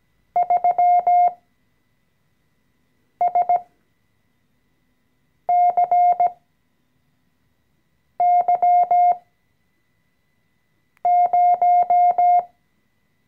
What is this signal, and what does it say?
Morse code, heard as a keyed tone, about 17 wpm, spaced as for 4 wpm, 710 Hz. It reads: 3SCY0
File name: CW-Online-Kurs / Anfänger / Lektion 014